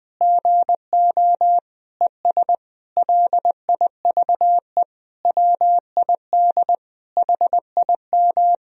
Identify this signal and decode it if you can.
Morse code, heard as a keyed tone, about 20 wpm, 700 Hz. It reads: GO ES LIVE WID HIM
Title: Sentences from Top 500 Words and Abbreviations 20wpm